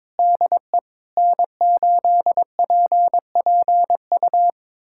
Morse code, heard as a keyed tone, about 21 wpm, 700 Hz. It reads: DE N8PPU
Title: USA Callsign DE Callsign - 5 Characters - 22wpm